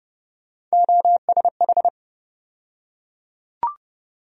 Morse code, heard as a keyed tone, about 30 wpm, 700 Hz. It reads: OSH
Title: USA Call Sign Postfixes - 3 Letters 30wpm